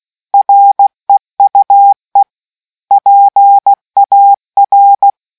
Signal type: Morse code